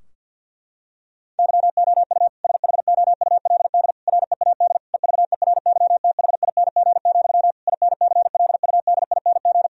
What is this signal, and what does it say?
Morse code, heard as a keyed tone, about 50 wpm, 700 Hz. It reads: YOU SHOULD READ EVERYTHING, INCLUDING
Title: Sentences from Top 1000 Words - Review Part 4 - 50wpm